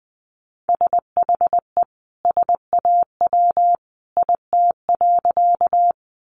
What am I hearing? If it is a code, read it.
Morse code, heard as a keyed tone, about 20 wpm, 700 Hz. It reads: SHE SAW IT.
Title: Sentences from Top 300 Words 20wpm